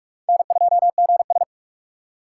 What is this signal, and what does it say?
Morse code, heard as a keyed tone, about 46 wpm, 700 Hz. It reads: N2GS
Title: USA Call Signs - 4 Characters - 3x Character Spacing - 45wpm